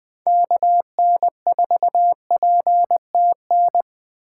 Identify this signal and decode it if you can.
Morse code, heard as a keyed tone, about 20 wpm, 700 Hz. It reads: KN4PTN
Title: USA Call Signs - 6 Characters - 4x Character Spacing - 20wpm